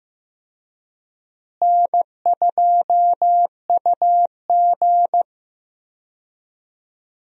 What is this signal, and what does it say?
Morse code, heard as a keyed tone, about 15 wpm, 700 Hz. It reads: N2UG
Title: USA Call Signs - 4 Characters - Repeated 3x - 15wpm